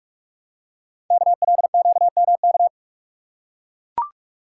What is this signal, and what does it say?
Morse code, heard as a keyed tone, about 44 wpm, 700 Hz. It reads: KLQMK
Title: Code Groups - 5 Letters - 45wpm